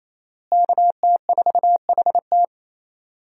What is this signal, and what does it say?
Morse code, heard as a keyed tone, about 28 wpm, 700 Hz. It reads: KT4HT
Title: USA Call Signs - 5 Characters - Repeated 2x - 28wpm